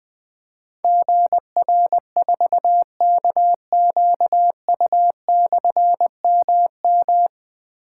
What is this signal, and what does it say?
Morse code, heard as a keyed tone, about 20 wpm, 700 Hz. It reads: GR4KQU/MM
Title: Difficult Call Signs 20wpm